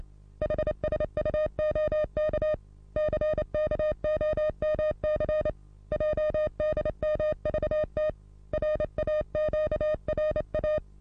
Morse code, heard as a keyed tone, about 26 wpm, 610 Hz. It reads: HSUOK CKOMC JDMVT RAQRA
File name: repos / Morsekurs / DJ4UF Morsen27